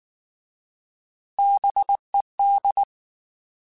Morse code, heard as a keyed tone, about 19 wpm, 800 Hz. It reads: BED
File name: Alphabetical / Alphabetical mp3s / Bs @18